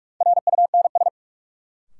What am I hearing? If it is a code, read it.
Morse code, heard as a keyed tone, about 44 wpm, 700 Hz. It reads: RUNS
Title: Sentences from Top 600 Words - New Word Drill - 45wpm